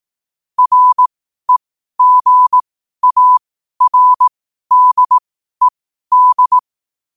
Morse code, heard as a keyed tone, about 18 wpm, spaced as for 12 wpm, 1000 Hz. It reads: REGARDED